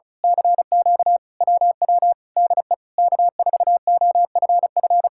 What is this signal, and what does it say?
Morse code, heard as a keyed tone, about 32 wpm, 700 Hz. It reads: CQ WW DE K4OFF
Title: Speed-Racing Common QSO Phrases Encore - 35wpm